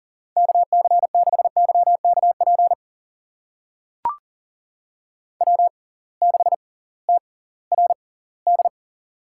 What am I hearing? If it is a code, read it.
Morse code, heard as a keyed tone, about 40 wpm, spaced as for 22 wpm, 700 Hz. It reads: KC6YKP W6TRD
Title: USA Call Signs - 6x Character Spacing - 40wpm